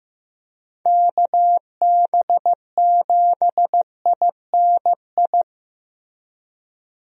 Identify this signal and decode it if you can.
Morse code, heard as a keyed tone, about 16 wpm, 700 Hz. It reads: KB7INI
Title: USA Call Signs - 6 Characters - Repeated 2x - 15wpm